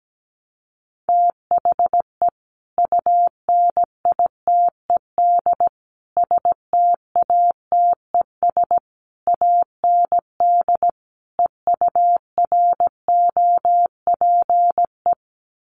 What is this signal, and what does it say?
Morse code, heard as a keyed tone, about 17 wpm, 700 Hz. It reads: THE UNITED STATES AND EUROPE
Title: Sets of 5 Words Encore 17wpm